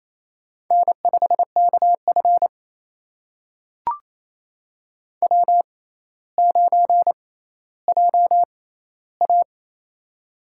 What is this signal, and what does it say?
Morse code, heard as a keyed tone, about 28 wpm, spaced as for 14 wpm, 700 Hz. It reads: N5KF W9JA
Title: USA Call Signs - 4 Characters - 6x Character Spacing - 28wpm